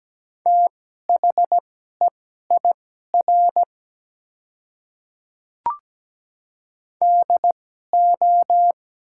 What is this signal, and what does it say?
Morse code, heard as a keyed tone, about 17 wpm, spaced as for 12 wpm, 700 Hz. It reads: THEIR DO
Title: 2x Character Spacing - Top 300 Words - 17wpm